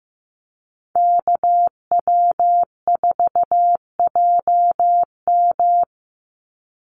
Morse code, heard as a keyed tone, about 15 wpm, 700 Hz. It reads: KW4JM
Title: NA SOTA Activator Callsigns 15wpm